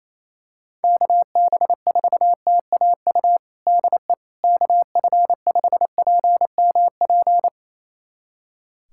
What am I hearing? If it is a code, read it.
Morse code, heard as a keyed tone, about 28 wpm, 700 Hz. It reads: KB4TAU DE KF5PMP
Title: USA Callsign DE Callsign - 6 Characters - 28wpm